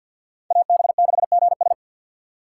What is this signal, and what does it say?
Morse code, heard as a keyed tone, about 52 wpm, 690 Hz. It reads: AB6GS
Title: USA Call Signs - 5 Characters - 2x Character Spacing - 50wpm